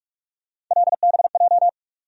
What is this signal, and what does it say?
Morse code, heard as a keyed tone, about 46 wpm, 700 Hz. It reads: LBJ
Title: USA Call Sign Postfixes - 3 Letters 45wpm